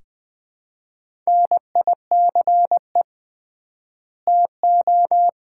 Morse code, heard as a keyed tone, about 20 wpm, 700 Hz. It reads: NICE TO
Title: Common QSO Phrases - 3x Word-Spacing - 20wpm